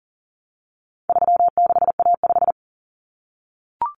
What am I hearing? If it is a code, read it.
Morse code, heard as a keyed tone, about 39 wpm, 700 Hz. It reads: 36A5
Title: Common Cut Numbers - 4-Digit 40wpm